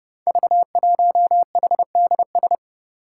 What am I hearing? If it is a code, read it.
Morse code, heard as a keyed tone, about 30 wpm, 700 Hz. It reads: V1HDS
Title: Five Challenging Letter-Number Combinations 30wpm